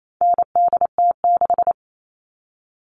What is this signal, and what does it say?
Morse code, heard as a keyed tone, about 28 wpm, 700 Hz. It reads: NDT6
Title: All Cut Numbers - 4-Digit 28wpm